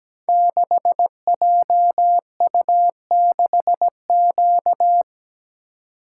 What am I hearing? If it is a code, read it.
Morse code, heard as a keyed tone, about 17 wpm, 700 Hz. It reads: 6JU6Q
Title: Code Groups - 5 Letter:Numbers - 17wpm